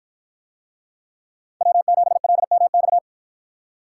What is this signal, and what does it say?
Morse code, heard as a keyed tone, about 49 wpm, 700 Hz. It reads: W7LMX